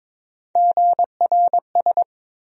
Morse code, heard as a keyed tone, about 22 wpm, 700 Hz. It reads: GRS
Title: USA Call Sign Postfixes - 3 Letters 22wpm